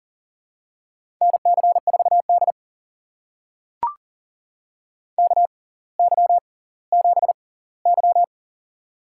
Morse code, heard as a keyed tone, about 40 wpm, spaced as for 21 wpm, 700 Hz. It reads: NC4D KY7Y
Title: USA Call Signs - 4 Characters - 6x Character Spacing - 40wpm